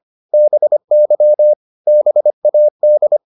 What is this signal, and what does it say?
Morse code, heard as a keyed tone, about 25 wpm, 600 Hz. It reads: BY BAD
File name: the old man and the sea part 1 25wpm